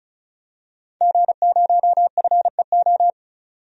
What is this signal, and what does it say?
Morse code, heard as a keyed tone, about 36 wpm, 700 Hz. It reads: G0FEO
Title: DXCC-100 Call Signs - 5 Characters - 35wpm